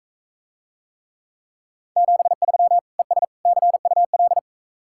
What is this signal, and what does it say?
Morse code, heard as a keyed tone, about 43 wpm, 700 Hz. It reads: 73 ES CUL